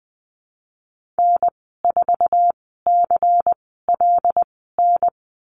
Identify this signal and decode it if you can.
Morse code, heard as a keyed tone, about 19 wpm, spaced as for 14 wpm, 700 Hz. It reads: N4CLN
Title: 147 Lesson 24g - Letter y - Call Signs 20wpm